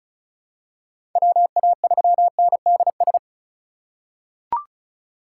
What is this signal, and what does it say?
Morse code, heard as a keyed tone, about 36 wpm, 700 Hz. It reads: WA3NDS
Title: USA Call Signs - 6 Characters - 2x Character Spacing - 35wpm